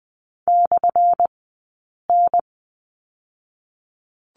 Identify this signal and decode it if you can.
Morse code, heard as a keyed tone, about 20 wpm, 700 Hz. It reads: / N
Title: 190 Lesson 31d - Letter v - Two Characters 20wpm